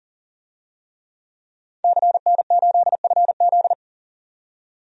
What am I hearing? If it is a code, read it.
Morse code, heard as a keyed tone, about 37 wpm, 700 Hz. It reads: CN8FZ
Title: DXCC-100 Call Signs - 5 Characters - 40wpm